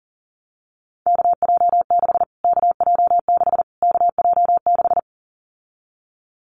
Morse code, heard as a keyed tone, about 40 wpm, 700 Hz. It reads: KJ6 KJ6 KJ6
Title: Callsign Prefixes for Top 25 DXCC - Repeated 3x - 40wpm